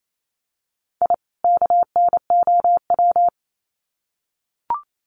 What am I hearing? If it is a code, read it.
Morse code, heard as a keyed tone, about 28 wpm, 700 Hz. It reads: I KNOW
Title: Sets of 2 Words 28wpm